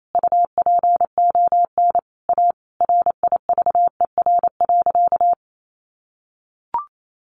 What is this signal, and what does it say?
Morse code, heard as a keyed tone, about 28 wpm, 700 Hz. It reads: UPON A RIVER.
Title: Sentences from Top 1100 Words 28wpm